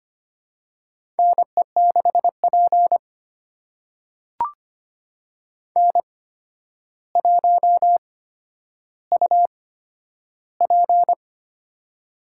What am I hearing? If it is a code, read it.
Morse code, heard as a keyed tone, about 25 wpm, spaced as for 11 wpm, 700 Hz. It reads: NE6P N1UP
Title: USA Call Signs - 8x Character Spacing - 25wpm